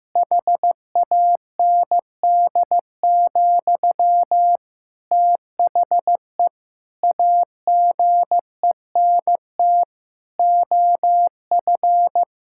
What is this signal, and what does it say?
Morse code, heard as a keyed tone, about 15 wpm, 700 Hz. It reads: HAND, THE AGENT OF
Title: Sentences from Top 1600 Words 15wpm